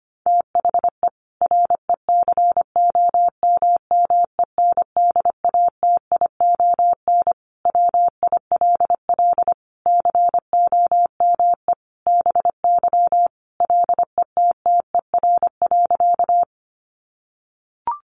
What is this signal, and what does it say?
Morse code, heard as a keyed tone, about 25 wpm, 700 Hz. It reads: THE RECOMMENDATION WILL COME BY LETTER.
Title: Sentences from Top 1400 Words 25wpm